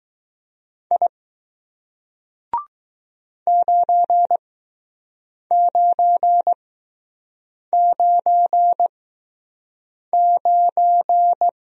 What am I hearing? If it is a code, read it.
Morse code, heard as a keyed tone, about 20 wpm, spaced as for 6 wpm, 700 Hz. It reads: I 9999